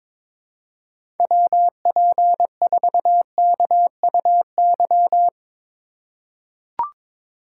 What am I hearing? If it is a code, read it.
Morse code, heard as a keyed tone, about 23 wpm, 700 Hz. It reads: WP4KUY
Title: USA Call Signs - 6x Character Spacing - 22wpm